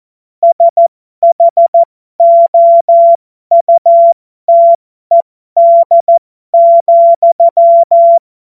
Morse code, heard as a keyed tone, about 14 wpm, spaced as for 12 wpm, 680 Hz. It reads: SHOUTED,
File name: Morsekurs / Witze